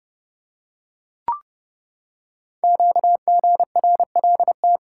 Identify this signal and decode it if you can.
Morse code, heard as a keyed tone, about 28 wpm, 700 Hz. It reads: QGRLT
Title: Code Groups - 5 Letters - 30wpm